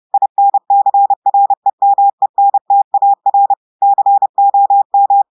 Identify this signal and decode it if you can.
Morse code, heard as a keyed tone, about 30 wpm, 800 Hz. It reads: INCREMENTAR COM
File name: audios / portugais / portugais texte en clair 2 vitesse 30 mots minutes